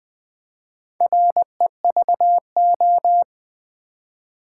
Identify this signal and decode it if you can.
Morse code, heard as a keyed tone, about 20 wpm, 700 Hz. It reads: REVO
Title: Build A Word - 4 Letters At A Time - 10-Letter Words - 20wpm